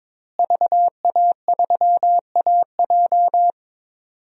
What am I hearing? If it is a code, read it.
Morse code, heard as a keyed tone, about 22 wpm, 700 Hz. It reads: VA3AJ
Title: DXCC-100 Call Signs - 5 Characters - Repeated 3x - 22wpm